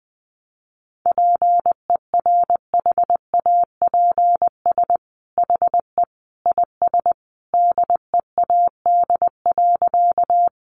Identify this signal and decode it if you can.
Morse code, heard as a keyed tone, about 20 wpm, 700 Hz. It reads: PERHAPS HE IS DEAD.
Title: Sentences from Top 1500 Words 20wpm